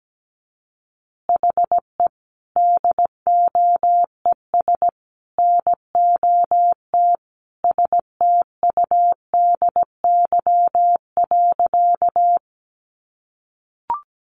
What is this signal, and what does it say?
Morse code, heard as a keyed tone, about 17 wpm, 700 Hz. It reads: HE DOES NOT STUDY.
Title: Additional Easy Phrases 17wpm